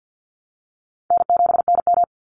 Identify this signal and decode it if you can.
Morse code, heard as a keyed tone, about 49 wpm, 700 Hz. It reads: N7NM
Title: Call Signs Repeated 2x - 50wpm